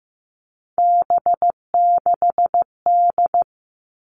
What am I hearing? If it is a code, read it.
Morse code, heard as a keyed tone, about 15 wpm, 700 Hz. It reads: B6D
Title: All Cut Numbers - 3-Digit 15wpm